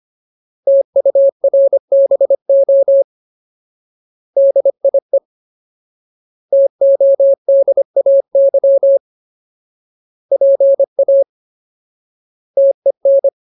Morse code, heard as a keyed tone, about 25 wpm, 550 Hz. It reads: TURBO DIE TODAY PA TEN
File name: Ultimate Word List / 25WPM Ultimate